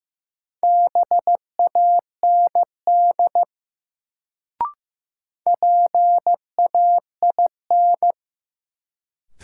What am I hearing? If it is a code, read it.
Morse code, heard as a keyed tone, about 15 wpm, 700 Hz. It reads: BAND PAIN
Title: Sentences from Top 1600 Words - New Word Drill - 15wpm